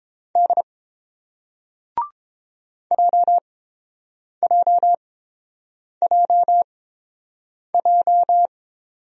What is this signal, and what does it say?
Morse code, heard as a keyed tone, about 31 wpm, spaced as for 7 wpm, 700 Hz. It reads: D JJJJ